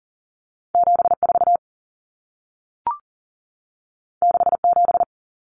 Morse code, heard as a keyed tone, about 40 wpm, 700 Hz. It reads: 74 67
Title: Numbers - 2-Digit 40wpm